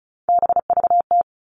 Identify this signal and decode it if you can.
Morse code, heard as a keyed tone, about 34 wpm, 700 Hz. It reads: BVT
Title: All Cut Numbers - 3-Digit 35wpm